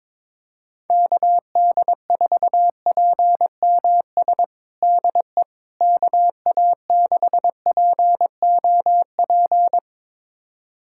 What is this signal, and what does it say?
Morse code, heard as a keyed tone, about 22 wpm, 700 Hz. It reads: KD4PMS DE KA6POP
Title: USA Callsign DE Callsign 22wpm